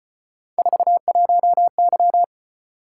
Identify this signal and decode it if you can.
Morse code, heard as a keyed tone, about 32 wpm, 700 Hz. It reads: 41Y